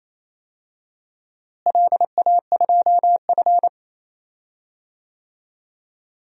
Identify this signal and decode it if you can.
Morse code, heard as a keyed tone, about 28 wpm, 700 Hz. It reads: LA2F
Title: DXCC-100 Call Signs - 4 Characters - Repeated 2x - 28wpm